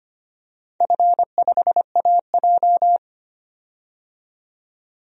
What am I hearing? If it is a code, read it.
Morse code, heard as a keyed tone, about 25 wpm, 700 Hz. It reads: F5AJ
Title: DXCC-100 Call Signs - 4 Characters - 25wpm